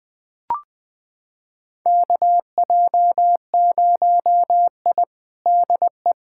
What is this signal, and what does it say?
Morse code, heard as a keyed tone, about 20 wpm, 700 Hz. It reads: KJ0I DE
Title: USA Callsign DE Callsign - 4 Characters - 20wpm